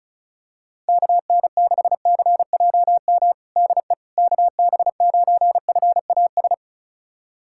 Morse code, heard as a keyed tone, about 36 wpm, 700 Hz. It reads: KN6CJM DE KB9FAS
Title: USA Callsign DE Callsign 35wpm